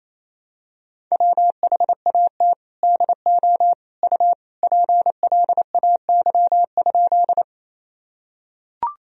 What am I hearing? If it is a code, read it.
Morse code, heard as a keyed tone, about 28 wpm, 700 Hz. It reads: WHAT DO U PLAY?
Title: Sentences from Top 200 Words and Abbreviations 28wpm